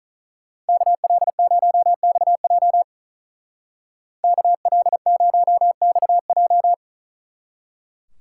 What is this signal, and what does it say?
Morse code, heard as a keyed tone, about 43 wpm, 700 Hz. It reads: KL0XJ KL0XJ
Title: Speed Racing - USA 2x2 Call Signs 35wpm